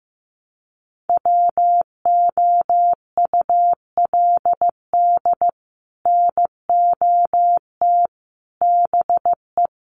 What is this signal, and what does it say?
Morse code, heard as a keyed tone, about 15 wpm, 700 Hz. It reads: WOULD NOT BE
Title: Sets of 5 Words 15wpm